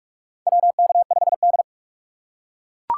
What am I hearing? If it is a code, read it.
Morse code, heard as a keyed tone, about 41 wpm, 700 Hz. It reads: WK5D
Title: USA Call Signs - 4x Character Spacing - 45wpm